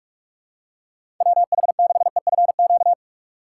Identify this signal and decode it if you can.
Morse code, heard as a keyed tone, about 46 wpm, 700 Hz. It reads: WH6EFQ
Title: USA Call Signs - 3x Character Spacing - 45wpm